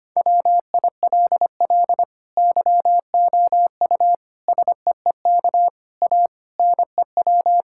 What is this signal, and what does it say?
Morse code, heard as a keyed tone, about 25 wpm, 700 Hz. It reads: WILL YOU SEEK A NEW
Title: Sentences from Top 1600 Words 25wpm